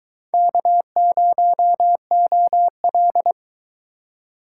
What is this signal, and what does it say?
Morse code, heard as a keyed tone, about 22 wpm, 700 Hz. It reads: K0OL